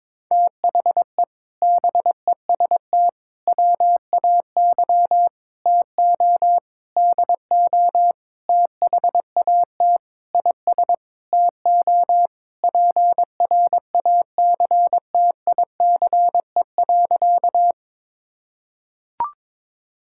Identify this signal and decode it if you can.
Morse code, heard as a keyed tone, about 22 wpm, 700 Hz. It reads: THE BEST WAY TO DO THAT IS TO PRACTICE.
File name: practice / 22wpm / Sentences with 8-Word Phrases - 22wpm